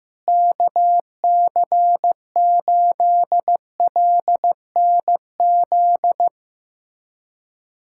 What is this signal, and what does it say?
Morse code, heard as a keyed tone, about 16 wpm, 700 Hz. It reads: KC8LNZ
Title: USA Callsign DE Callsign - 6 Characters - 15wpm